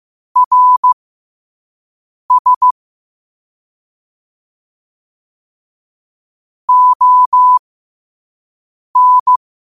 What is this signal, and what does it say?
Morse code, heard as a keyed tone, about 15 wpm, spaced as for 5 wpm, 1000 Hz. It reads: RS ON
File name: qotdfiles / 5wpm